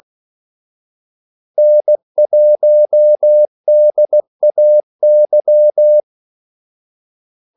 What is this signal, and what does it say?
Morse code, heard as a keyed tone, about 16 wpm, 600 Hz. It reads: N1DAY
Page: Kaldesignaler 16 wpm